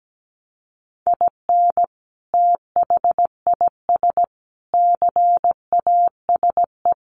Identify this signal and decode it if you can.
Morse code, heard as a keyed tone, about 17 wpm, 700 Hz. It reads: IN THIS CASE
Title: Sentences from Top 1400 Words 17wpm